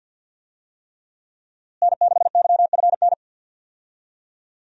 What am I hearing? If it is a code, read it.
Morse code, heard as a keyed tone, about 51 wpm, 700 Hz. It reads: N6YFN
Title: USA Call Signs - 5 Characters - Repeated 3x - 50wpm